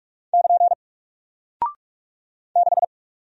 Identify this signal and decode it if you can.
Morse code, heard as a keyed tone, about 43 wpm, 700 Hz. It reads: <KN> 6
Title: Single Character (Letter:Number:Punctuation:Prosign) Alt 45wpm